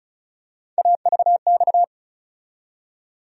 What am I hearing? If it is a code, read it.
Morse code, heard as a keyed tone, about 36 wpm, 700 Hz. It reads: AVX
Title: USA Call Sign Postfixes - 3 Letters 35wpm